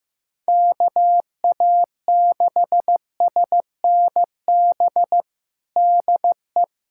Morse code, heard as a keyed tone, about 15 wpm, 700 Hz. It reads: KA6SNB DE
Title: USA Callsign DE Callsign - 6 Characters - 15wpm